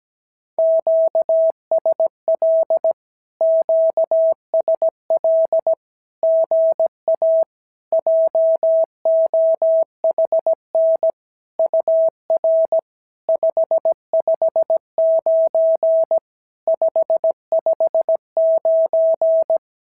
Morse code, heard as a keyed tone, about 17 wpm, 650 Hz. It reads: QSL QSL GA JOHN UR 559 559
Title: SOTA QSO Practice 17wpm